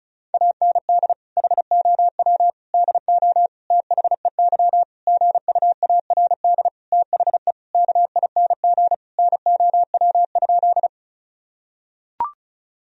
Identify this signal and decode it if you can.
Morse code, heard as a keyed tone, about 36 wpm, 700 Hz. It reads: AND HOW DO THEY GUARD THE KING NOW?
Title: Sentences from Top 1000 Words - Review Part 4 - 35wpm